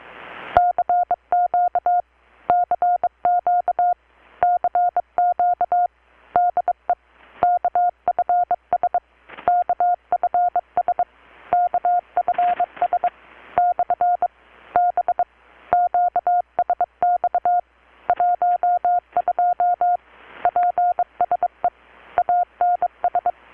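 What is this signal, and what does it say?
Morse code, heard as a keyed tone, about 23 wpm, 720 Hz. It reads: CQ CQ CQ DE KFS KFS KFS / B QSX 12 PSE ANS
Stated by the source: Morse from KPH's Night of Nights XX maritime radiotelegraphy event, received in Chiapas, Mexico about 3,800 km away on a Kenwood R-600 with a grounded 10 m long-wire antenna.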